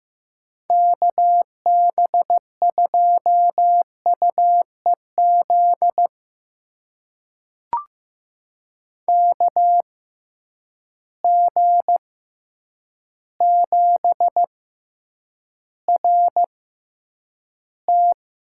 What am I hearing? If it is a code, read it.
Morse code, heard as a keyed tone, about 15 wpm, spaced as for 8 wpm, 700 Hz. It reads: KB2UEZ KG7RT
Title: USA Call Signs - 6 Characters - 6x Character Spacing - 15wpm